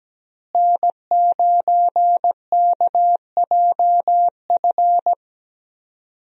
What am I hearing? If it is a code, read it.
Morse code, heard as a keyed tone, about 17 wpm, 700 Hz. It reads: N9KJF
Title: USA Call Signs - 5 Characters - Repeated 3x - 17wpm